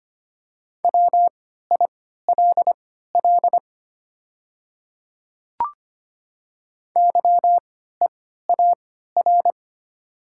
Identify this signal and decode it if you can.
Morse code, heard as a keyed tone, about 25 wpm, spaced as for 14 wpm, 690 Hz. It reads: WILL YEAR
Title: 3x Character Spacing - Top 300 Words - 25wpm